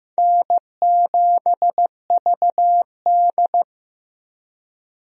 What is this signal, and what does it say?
Morse code, heard as a keyed tone, about 15 wpm, 700 Hz. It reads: N7VD
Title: USA Call Signs - 4 Characters - 15wpm